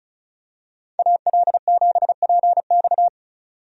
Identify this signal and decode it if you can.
Morse code, heard as a keyed tone, about 36 wpm, 700 Hz. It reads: AL7PX
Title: USA Call Signs - 5 Characters - 35wpm